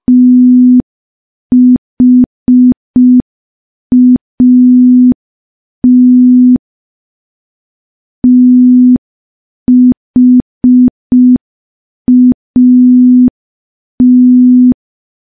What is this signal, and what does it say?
Morse code, heard as a keyed tone, about 5 wpm, 250 Hz. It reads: THAT THAT
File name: morse / morse-training-05WPM